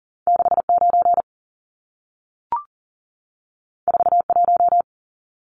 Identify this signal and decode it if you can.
Morse code, heard as a keyed tone, about 40 wpm, 700 Hz. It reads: 69 41
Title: Numbers - 2-Digit 40wpm